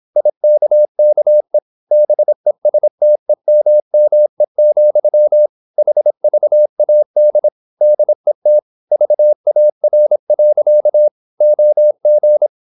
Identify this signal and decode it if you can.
Morse code, heard as a keyed tone, about 26 wpm, 600 Hz. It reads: IKKE BESTEMME, HVAD DET VAR. OG
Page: Ferie 26 wpm